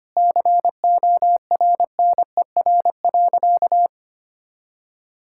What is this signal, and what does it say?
Morse code, heard as a keyed tone, about 25 wpm, 700 Hz. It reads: CORNER.
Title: Sentences from Top 1600 Words 25wpm